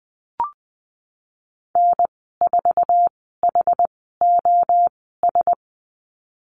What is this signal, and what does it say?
Morse code, heard as a keyed tone, about 19 wpm, spaced as for 14 wpm, 700 Hz. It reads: N4HOS
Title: 076 Lesson 13g - Letter L - Call Signs 20wpm